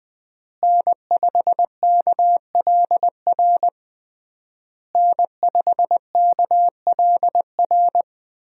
Morse code, heard as a keyed tone, about 20 wpm, 700 Hz. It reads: N5KLR N5KLR
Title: USA Call Signs - 5 Characters - Repeated 3x - 20wpm